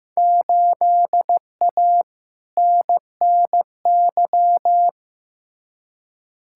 Morse code, heard as a keyed tone, about 15 wpm, 700 Hz. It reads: 8A NNY
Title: Field Day Exchanges Encore - 15wpm